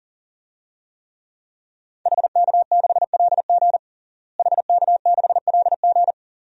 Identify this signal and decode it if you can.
Morse code, heard as a keyed tone, about 40 wpm, 700 Hz. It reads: HK6LG HK6LG
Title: DXCC-100 Call Signs - 5 Characters - Repeated 3x - 40wpm